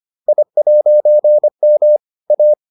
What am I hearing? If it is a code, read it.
Morse code, heard as a keyed tone, about 25 wpm, 600 Hz. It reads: I'M A